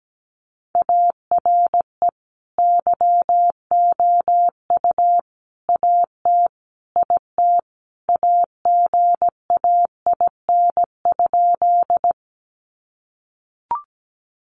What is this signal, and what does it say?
Morse code, heard as a keyed tone, about 17 wpm, 700 Hz. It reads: ARE YOU AT IT AGAIN?
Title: Sentences from Top 200 Words 17wpm